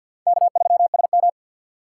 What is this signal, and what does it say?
Morse code, heard as a keyed tone, about 52 wpm, 700 Hz. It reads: K3SM
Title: USA Call Signs - 4 Characters - 6x Character Spacing - 50wpm